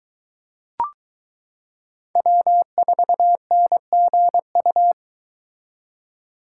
Morse code, heard as a keyed tone, about 22 wpm, 700 Hz. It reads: W4NGU